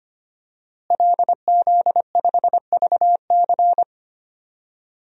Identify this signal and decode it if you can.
Morse code, heard as a keyed tone, about 25 wpm, 700 Hz. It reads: LZ5VC
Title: DXCC-100 Call Signs - 5 Characters - Repeated 3x - 25wpm